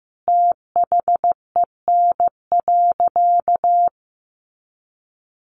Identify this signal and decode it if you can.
Morse code, heard as a keyed tone, about 15 wpm, 700 Hz. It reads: THEN.
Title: Sentences from Top 200 Words 15wpm